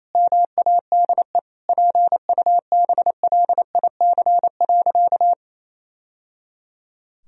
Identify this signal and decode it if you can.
Morse code, heard as a keyed tone, about 28 wpm, 700 Hz. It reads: MADE PUBLIC.
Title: Sentences from Top 1000 Words - Review Part 3 - 28wpm